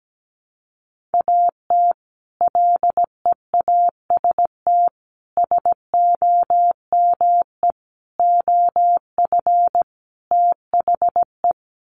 Morse code, heard as a keyed tone, about 17 wpm, 700 Hz. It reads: AT LEAST SOME OF THE
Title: Sets of 5 Words Encore 17wpm